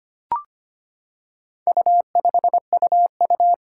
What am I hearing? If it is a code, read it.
Morse code, heard as a keyed tone, about 25 wpm, 700 Hz. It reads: U5UU
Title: Four Challenging Letter-Number Combinations 25wpm